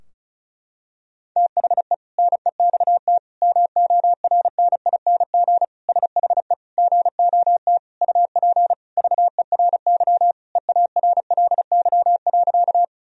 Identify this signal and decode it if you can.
Morse code, heard as a keyed tone, about 34 wpm, 700 Hz. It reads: THE NEXT MORNING SHE GOT UP VERY EARLY.
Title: Sentences from Top 1000 Words - Review Part 2 - 35wpm